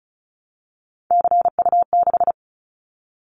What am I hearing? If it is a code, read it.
Morse code, heard as a keyed tone, about 34 wpm, 700 Hz. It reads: CU6
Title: Callsign Prefixes for Top 100 DXCC - 35wpm